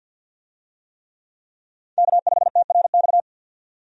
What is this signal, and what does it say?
Morse code, heard as a keyed tone, about 52 wpm, 700 Hz. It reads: K5TRX
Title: USA Call Signs - 5 Characters - 50wpm